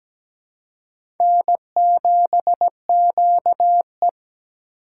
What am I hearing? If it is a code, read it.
Morse code, heard as a keyed tone, about 17 wpm, 700 Hz. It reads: N7QE
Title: USA Call Signs - 4 Characters - 3x Character Spacing - 17wpm